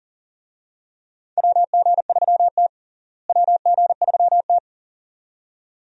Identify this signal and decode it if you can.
Morse code, heard as a keyed tone, about 40 wpm, 700 Hz. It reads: WG3T WG3T
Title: USA Call Signs - 4 Characters - Repeated 2x - 40wpm